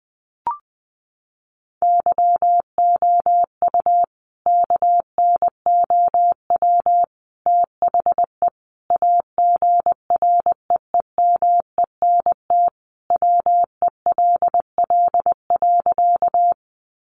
Morse code, heard as a keyed tone, about 20 wpm, 700 Hz. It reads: YOU KNOW THE AGREEMENT WELL.
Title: Sentences from Top 1500 Words 20wpm